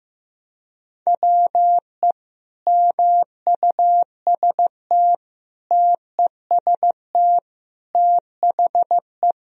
Morse code, heard as a keyed tone, about 14 wpm, 700 Hz. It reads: WE MUST TEST THE
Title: Sentences from Top 1000 Words - Review Part 3 - 15wpm